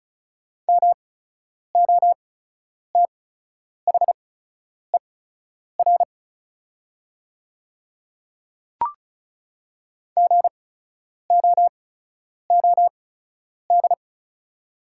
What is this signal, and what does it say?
Morse code, heard as a keyed tone, about 36 wpm, spaced as for 10 wpm, 700 Hz. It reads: MOTHER GOOD
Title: 8x Character Spacing - Top 200 Words - 35wpm